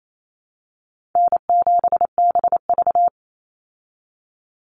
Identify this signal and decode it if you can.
Morse code, heard as a keyed tone, about 28 wpm, 700 Hz. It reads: N7BV
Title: Call Signs Repeated 2x - 28wpm